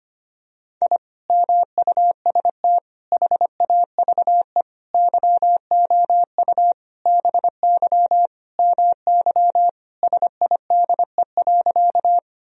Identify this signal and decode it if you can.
Morse code, heard as a keyed tone, about 25 wpm, 700 Hz. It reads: I MUST HAVE YOU BY MY SIDE.
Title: Sentences from Top 1000 Words - Review Part 1 - 25wpm